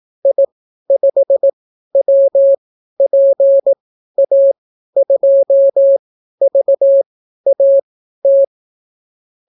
Morse code, heard as a keyed tone, about 18 wpm, spaced as for 12 wpm, 550 Hz. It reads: I5WPA2VAT